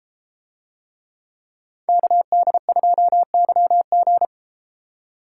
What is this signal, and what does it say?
Morse code, heard as a keyed tone, about 33 wpm, 700 Hz. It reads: KD2YG